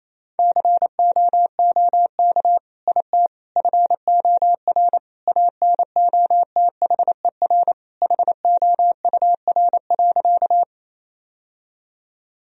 Morse code, heard as a keyed tone, about 28 wpm, 700 Hz. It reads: COOK IT FOR ANOTHER HOUR.
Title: Sentences from Top 1000 Words - Review Part 4 - 28wpm